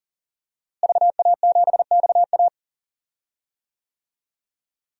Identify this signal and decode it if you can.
Morse code, heard as a keyed tone, about 40 wpm, 700 Hz. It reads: VA7XA
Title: DXCC-100 Call Signs - 5 Characters - Repeated 3x - 40wpm